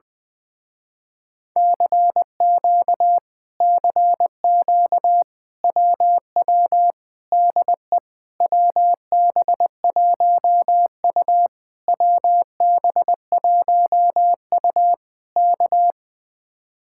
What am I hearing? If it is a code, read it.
Morse code, heard as a keyed tone, about 20 wpm, 700 Hz. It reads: CQ CQ WW DE WB1U WB1U K
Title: Speed-Racing Common QSO Phrases - 17wpm